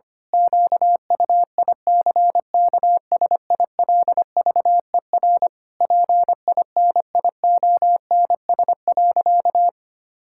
Morse code, heard as a keyed tone, about 25 wpm, 700 Hz. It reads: QUICKSILVER PINIONS.
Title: Pangram Challenge v2 - 25wpm